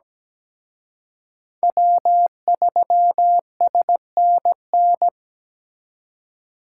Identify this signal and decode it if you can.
Morse code, heard as a keyed tone, about 20 wpm, 700 Hz. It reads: W3SNN
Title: Speed Racing - USA 1x3 Call Signs 17wpm